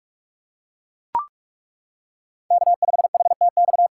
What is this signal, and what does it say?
Morse code, heard as a keyed tone, about 41 wpm, 700 Hz. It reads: K5HTX
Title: USA Callsign DE Callsign - 5 Characters - 45wpm